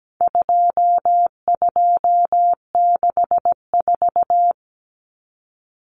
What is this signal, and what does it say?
Morse code, heard as a keyed tone, about 17 wpm, 700 Hz. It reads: 2264
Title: Numbers - 4-Digit 17wpm